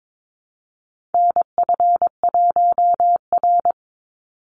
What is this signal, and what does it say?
Morse code, heard as a keyed tone, about 22 wpm, 700 Hz. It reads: NF1R
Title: NA SOTA Chaser Callsigns 22wpm